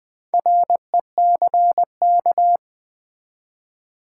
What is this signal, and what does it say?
Morse code, heard as a keyed tone, about 20 wpm, 700 Hz. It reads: RECK
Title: Build A Word - 2 Letters At A Time - 6-Letter Words - 20wpm